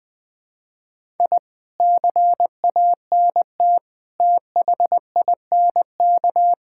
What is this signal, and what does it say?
Morse code, heard as a keyed tone, about 21 wpm, 700 Hz. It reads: I CANT THINK
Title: Sentences with 8-Word Phrases - 20wpm